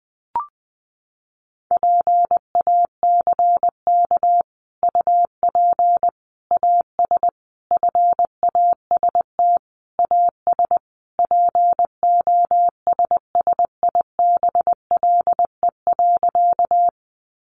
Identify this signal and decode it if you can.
Morse code, heard as a keyed tone, about 20 wpm, 700 Hz. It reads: PACK UP AS FAST AS POSSIBLE.
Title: Sentences from Top 1300 Words 20wpm